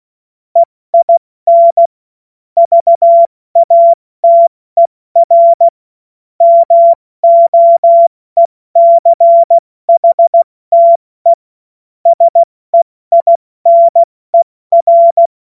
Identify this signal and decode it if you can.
Morse code, heard as a keyed tone, about 16 wpm, spaced as for 14 wpm, 680 Hz. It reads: EIN VATER MOECHTE SEINER
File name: Morsekurs / Witze / Witz 31. Januar 2023 70 F80